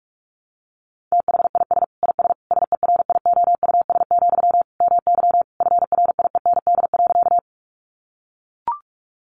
Morse code, heard as a keyed tone, about 44 wpm, 700 Hz. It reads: THIS IS SERIOUS, MY FRIEND.
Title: Sentences from Top 1100 Words 45wpm